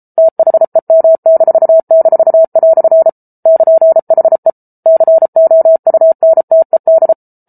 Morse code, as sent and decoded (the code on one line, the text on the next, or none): - .... -- -....- -....- .-..-. / -.--. .... / -.-. --- ..- -. - . -..
THEM--" <KN>HE COUNTED